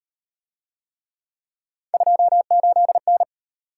Morse code, heard as a keyed tone, about 35 wpm, 700 Hz. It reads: 28N